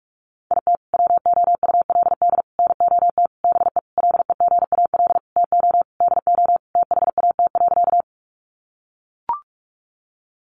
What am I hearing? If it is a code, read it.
Morse code, heard as a keyed tone, about 44 wpm, 700 Hz. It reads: IT WOULD NOT BE LEGAL TO DO THAT.
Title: Sentences from Top 1200 Words 45wpm